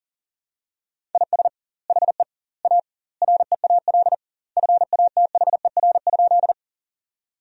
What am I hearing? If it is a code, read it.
Morse code, heard as a keyed tone, about 40 wpm, 700 Hz. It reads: IS HE A REAL FATHER?
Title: 2x Word-Spacing - Sentences from Top 300 Words - 40wpm